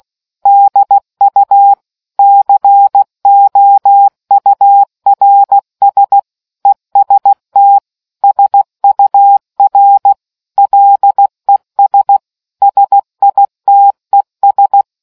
Morse code, heard as a keyed tone, about 16 wpm, 800 Hz. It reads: DU COURS EST SUR LES SITES